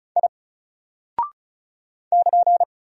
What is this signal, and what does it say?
Morse code, heard as a keyed tone, about 36 wpm, 700 Hz. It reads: I <KN>
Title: Single Character (Letter:Number:Punctuation:Prosign) Alt 35wpm